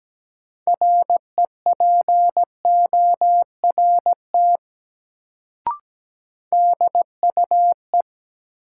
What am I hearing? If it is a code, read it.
Morse code, heard as a keyed tone, about 17 wpm, 700 Hz. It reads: REPORT DUE
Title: Sentences from Top 900 Words - New Word Drill - 17wpm